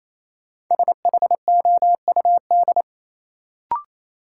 Morse code, heard as a keyed tone, about 27 wpm, 700 Hz. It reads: SHOUD
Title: Sentences from Top 1000 Words - Review Part 4 - New Word Drill - 28wpm